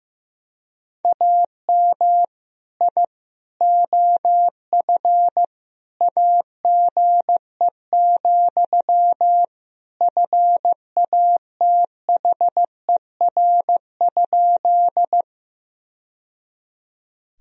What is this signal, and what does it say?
Morse code, heard as a keyed tone, about 16 wpm, 700 Hz. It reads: AM I OF AGE, FATHER?
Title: Sentences from Top 1000 Words - Review Part 3 - 15wpm